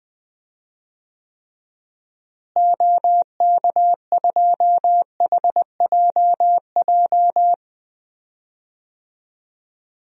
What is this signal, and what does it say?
Morse code, heard as a keyed tone, about 20 wpm, 700 Hz. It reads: OK2HJJ
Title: DXCC-100 Call Signs - 6 Characters - Repeated 2x - 20wpm